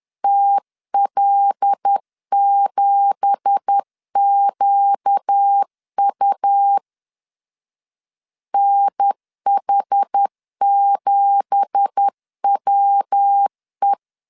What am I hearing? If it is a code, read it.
Morse code, heard as a keyed tone, about 10 wpm, 800 Hz. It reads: TL7QU NH7WE
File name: LEZIONI / 9 2867 10wpm 2733